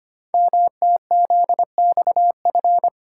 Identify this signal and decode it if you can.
Morse code, heard as a keyed tone, about 25 wpm, 700 Hz. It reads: MTZXF
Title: Code Groups - 5 Letters - 25wpm